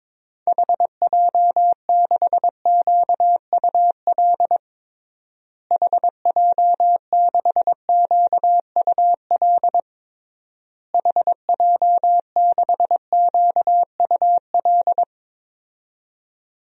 Morse code, heard as a keyed tone, about 22 wpm, 700 Hz. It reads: HJ6QUL HJ6QUL HJ6QUL
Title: DXCC-100 Call Signs - 6 Characters - Repeated 3x - 22wpm